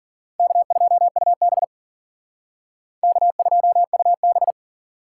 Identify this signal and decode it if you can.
Morse code, heard as a keyed tone, about 52 wpm, 700 Hz. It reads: K2UB K2UB